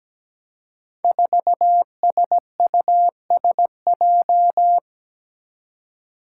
Morse code, heard as a keyed tone, about 17 wpm, 700 Hz. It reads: 4SUSJ
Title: Five Challenging Letter-Number Combinations 17wpm